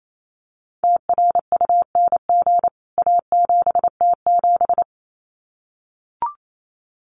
Morse code, heard as a keyed tone, about 28 wpm, 700 Hz. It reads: TRUNG A7T7
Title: CWT Exchange 28wpm